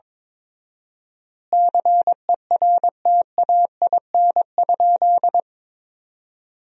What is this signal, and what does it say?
Morse code, heard as a keyed tone, about 22 wpm, 700 Hz. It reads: CERTAIN?
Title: 4x Word-Spacing - Sentences from Top 500 Words - 22wpm